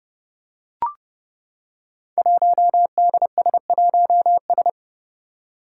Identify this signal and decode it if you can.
Morse code, heard as a keyed tone, about 30 wpm, 700 Hz. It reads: 1DS1S
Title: Five Challenging Letter-Number Combinations 30wpm